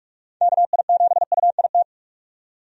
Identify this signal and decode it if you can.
Morse code, heard as a keyed tone, about 45 wpm, 700 Hz. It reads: KI7UIT
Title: USA Call Signs - 6 Characters - 45wpm